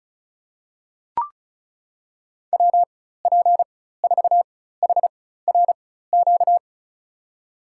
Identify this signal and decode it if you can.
Morse code, heard as a keyed tone, about 36 wpm, spaced as for 16 wpm, 700 Hz. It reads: WP4HRQ
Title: USA Call Signs - 6 Characters - 4x Character Spacing - 35wpm